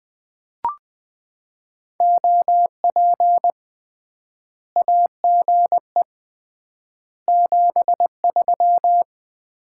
Morse code, heard as a keyed tone, about 21 wpm, 700 Hz. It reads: OP AGE 73
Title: Common QSO Phrases - 3x Word-Spacing - 20wpm